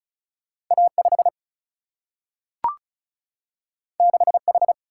Morse code, heard as a keyed tone, about 36 wpm, 700 Hz. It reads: A5 6H
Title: Single-Letter Single-Number 35wpm